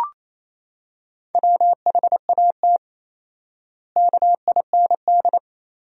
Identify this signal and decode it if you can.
Morse code, heard as a keyed tone, about 28 wpm, 700 Hz. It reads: WHAT KIND
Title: 4x Word-Spacing - Sentences from Top 500 Words - 28wpm